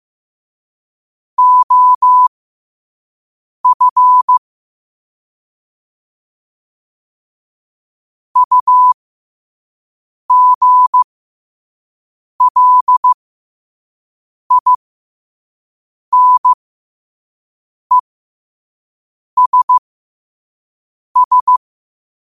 Morse code, heard as a keyed tone, about 15 wpm, spaced as for 5 wpm, 1000 Hz. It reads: OF UGLINESS
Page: New quotes every day in morse code at 5 Words per minute.